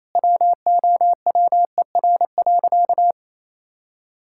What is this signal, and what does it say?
Morse code, heard as a keyed tone, about 28 wpm, 700 Hz. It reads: WOWER.
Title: Pangram Challenge v2 - 28wpm